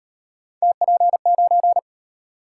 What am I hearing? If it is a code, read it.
Morse code, heard as a keyed tone, about 38 wpm, 700 Hz. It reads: TP9